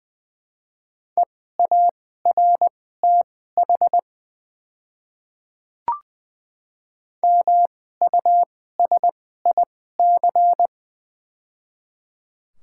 Morse code, heard as a keyed tone, about 20 wpm, spaced as for 15 wpm, 700 Hz. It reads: EARTH MUSIC
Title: 2x Character Spacing - Top 500 Words - 20wpm